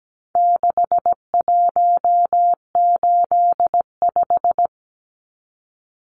Morse code, heard as a keyed tone, about 17 wpm, 700 Hz. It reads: 6185
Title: Numbers - 4-Digit 17wpm